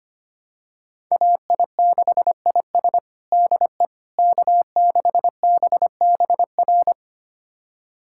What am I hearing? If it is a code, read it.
Morse code, heard as a keyed tone, about 25 wpm, 700 Hz. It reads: AI6IS DE K6BBR
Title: USA Callsign DE Callsign - 5 Characters - 25wpm